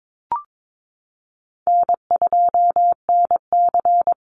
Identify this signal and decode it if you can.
Morse code, heard as a keyed tone, about 22 wpm, 700 Hz. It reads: N2NC
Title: Call Signs Repeated 2x - 22wpm